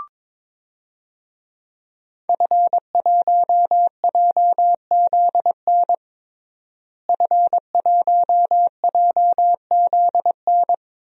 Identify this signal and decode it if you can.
Morse code, heard as a keyed tone, about 22 wpm, 700 Hz. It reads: F1JZN F1JZN
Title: DXCC-100 Call Signs - 5 Characters - Repeated 2x - 22wpm